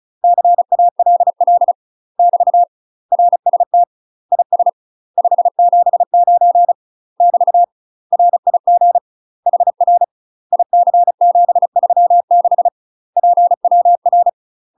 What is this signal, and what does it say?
Morse code, heard as a keyed tone, about 36 wpm, 700 Hz. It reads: CALL <BT> RST IS 579 <BT> RIG HR IC736 PWR
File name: UK MASTER QSOs TRAINING / 35WPM / UK MASTER QSOs 0031